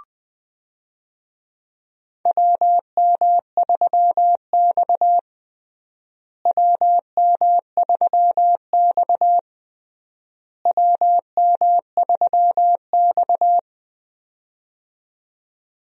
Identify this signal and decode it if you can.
Morse code, heard as a keyed tone, about 20 wpm, 700 Hz. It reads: WM3X WM3X WM3X
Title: USA Call Signs - 4 Characters - Repeated 3x - 20wpm